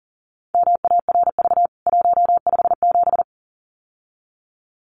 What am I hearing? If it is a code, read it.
Morse code, heard as a keyed tone, about 40 wpm, 700 Hz. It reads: MARV 157
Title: CWT Exchange 40wpm